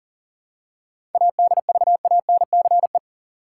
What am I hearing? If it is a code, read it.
Morse code, heard as a keyed tone, about 40 wpm, 700 Hz. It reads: ADVANCE
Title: Seven Letter Words - 40wpm